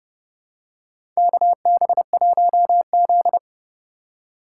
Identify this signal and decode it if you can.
Morse code, heard as a keyed tone, about 30 wpm, 700 Hz. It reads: KB1Z
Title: USA Call Signs - 4 Characters - 4x Character Spacing - 30wpm